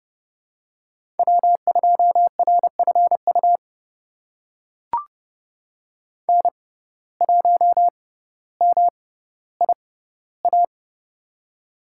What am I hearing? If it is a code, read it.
Morse code, heard as a keyed tone, about 30 wpm, spaced as for 15 wpm, 700 Hz. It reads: W2RFU N1MIA
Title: USA Call Signs - 5 Characters - 6x Character Spacing - 30wpm